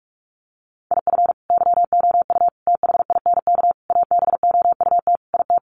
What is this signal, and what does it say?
Morse code, heard as a keyed tone, about 44 wpm, 700 Hz. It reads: IF YOU THINK ABOUT IT
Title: Sets of 5 Words Encore2 45wpm